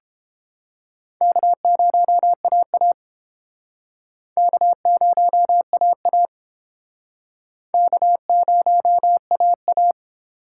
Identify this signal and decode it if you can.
Morse code, heard as a keyed tone, about 31 wpm, 700 Hz. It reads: K0AA K0AA K0AA